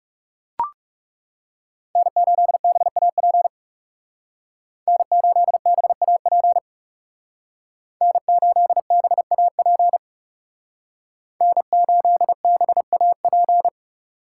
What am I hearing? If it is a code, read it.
Morse code, heard as a keyed tone, about 36 wpm, 700 Hz. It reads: N8BAP N8BAP N8BAP N8BAP
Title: Speed Racing - USA 1x3 Call Signs 30wpm